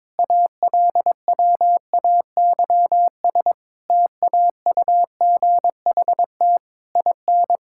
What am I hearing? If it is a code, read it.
Morse code, heard as a keyed tone, about 22 wpm, 700 Hz. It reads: ALWAYS TAUGHT IN
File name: Difficult Sentences with Hints - 22wpm